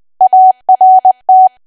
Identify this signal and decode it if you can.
Morse code, heard as a keyed tone, about 20 wpm, 750 Hz. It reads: ART
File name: cwa / W1AW-Level3 / 20WPM